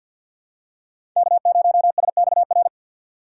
Morse code, heard as a keyed tone, about 53 wpm, 700 Hz. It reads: K0SXR
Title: USA Call Signs - 4x Character Spacing - 50wpm